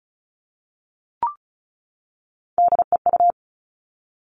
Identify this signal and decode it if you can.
Morse code, heard as a keyed tone, about 36 wpm, 700 Hz. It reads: DEU
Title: Three Challenging Letter-Number Combinations 35wpm